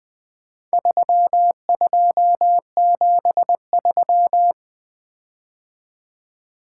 Morse code, heard as a keyed tone, about 23 wpm, 700 Hz. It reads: 3273
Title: Speed-Racing - Four Letter-Number Combinations - 20wpm